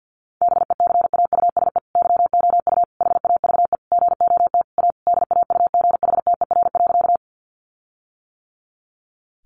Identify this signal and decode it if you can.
Morse code, heard as a keyed tone, about 50 wpm, 700 Hz. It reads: BECAUSE YOU HAVE GOT A DAUGHTER.
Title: Sentences from Top 1500 Words 50wpm